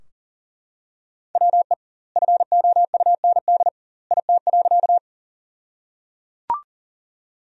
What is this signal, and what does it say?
Morse code, heard as a keyed tone, about 37 wpm, 700 Hz. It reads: WE FOUND IT.
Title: 2x Word-Spacing - Sentences from Top 300 Words - 40wpm